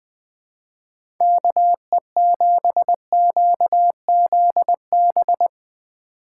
Morse code, heard as a keyed tone, about 21 wpm, 700 Hz. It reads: KE7QZB
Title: USA Call Signs - 6 Characters - 2x Character Spacing - 20wpm